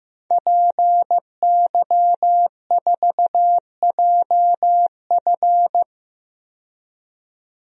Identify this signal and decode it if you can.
Morse code, heard as a keyed tone, about 15 wpm, 700 Hz. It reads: PY4JF
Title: DXCC-100 Call Signs - 5 Characters - 15wpm